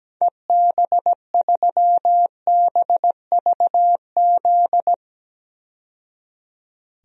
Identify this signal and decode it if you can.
Morse code, heard as a keyed tone, about 17 wpm, 700 Hz. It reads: EB3BVZ
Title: DXCC-100 Call Signs - 6 Characters - 17wpm